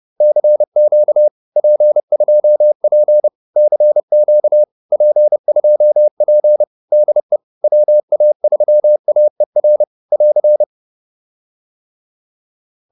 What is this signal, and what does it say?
Morse code, heard as a keyed tone, about 30 wpm, 590 Hz. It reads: CQ P2P CQ P2P DE WA3AER <AR>